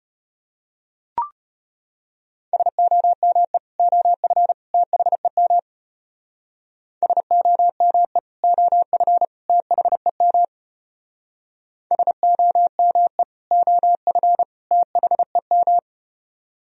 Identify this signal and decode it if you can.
Morse code, heard as a keyed tone, about 33 wpm, 700 Hz. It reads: SOME OF THEM SOME OF THEM SOME OF THEM